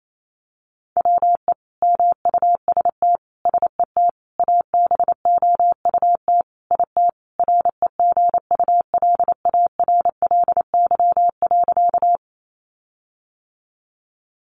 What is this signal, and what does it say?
Morse code, heard as a keyed tone, about 28 wpm, 700 Hz. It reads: WE MUST SET ABOUT IT REGULARLY.
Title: Sentences from Top 1400 Words 28wpm